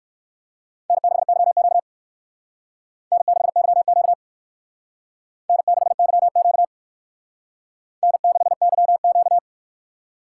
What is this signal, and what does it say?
Morse code, heard as a keyed tone, about 56 wpm, 700 Hz. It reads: N6YQ N6YQ N6YQ N6YQ